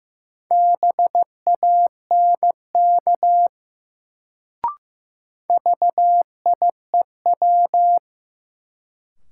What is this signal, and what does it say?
Morse code, heard as a keyed tone, about 15 wpm, 700 Hz. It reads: BANK VIEW
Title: Sentences from Top 900 Words - New Word Drill - 15wpm